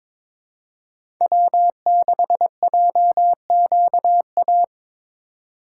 Morse code, heard as a keyed tone, about 22 wpm, 700 Hz. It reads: W6JQA
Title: USA Call Signs - 5 Characters - 2x Character Spacing - 22wpm